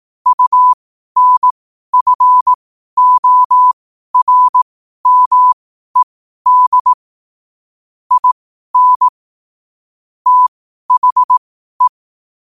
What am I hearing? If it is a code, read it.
Morse code, heard as a keyed tone, about 18 wpm, spaced as for 12 wpm, 1000 Hz. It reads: UNFORMED IN THE
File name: 12wpm